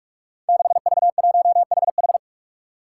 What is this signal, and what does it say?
Morse code, heard as a keyed tone, about 42 wpm, 700 Hz. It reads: 6V1HH
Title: Five Challenging Letter-Number Combinations 45wpm